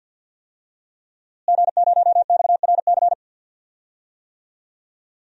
Morse code, heard as a keyed tone, about 47 wpm, 700 Hz. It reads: G0XRC
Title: DXCC-100 Call Signs - 5 Characters - Repeated 2x - 50wpm